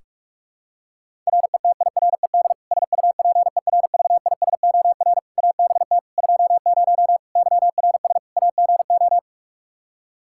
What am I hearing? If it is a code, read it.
Morse code, heard as a keyed tone, about 46 wpm, 700 Hz. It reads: RETIRED SUPERVISOR ABT 20 YRS AGO
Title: Practice QSO - Exchange 3 - More Info - 45wpm